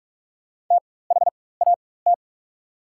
Morse code, heard as a keyed tone, about 44 wpm, spaced as for 21 wpm, 700 Hz. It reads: THAT
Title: 4x Character Spacing - Top 200 Words - 45wpm